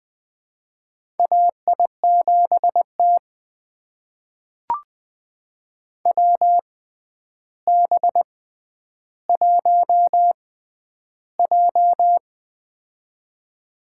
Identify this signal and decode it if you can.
Morse code, heard as a keyed tone, about 20 wpm, spaced as for 10 wpm, 700 Hz. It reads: AI7T WB1J
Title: USA Call Signs - 4 Characters - 6x Character Spacing - 20wpm